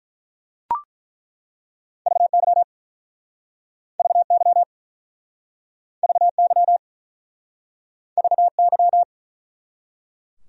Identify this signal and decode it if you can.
Morse code, heard as a keyed tone, about 48 wpm, 700 Hz. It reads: VY VY VY VY